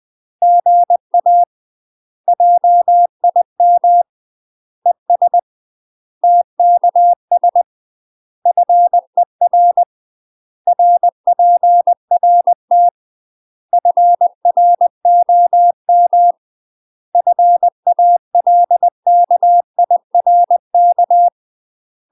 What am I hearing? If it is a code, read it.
Morse code, heard as a keyed tone, about 20 wpm, 700 Hz. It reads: GA JIM ES TKS FER RPRT FROM FALKIRK
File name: UK MASTER QSOs TRAINING / 20WPM / UK MASTER QSOs 0070